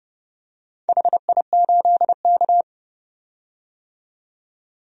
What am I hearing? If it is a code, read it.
Morse code, heard as a keyed tone, about 29 wpm, 700 Hz. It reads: HI8K
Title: DXCC-100 Call Signs - 4 Characters - 30wpm